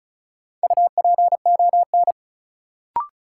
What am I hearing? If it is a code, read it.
Morse code, heard as a keyed tone, about 36 wpm, 700 Hz. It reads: UPON
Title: Sentences from Top 1000 Words - Review Part 6 - New Word Drill - 35wpm